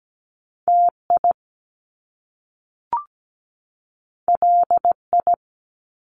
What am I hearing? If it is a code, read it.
Morse code, heard as a keyed tone, about 17 wpm, 700 Hz. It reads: TI LI
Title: Top Two Letter Combinations in Words 17wpm